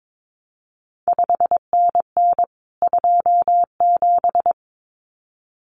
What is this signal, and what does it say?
Morse code, heard as a keyed tone, about 22 wpm, 700 Hz. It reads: 5NN 27
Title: CQ WW DX Contest Exchange Encore - 22wpm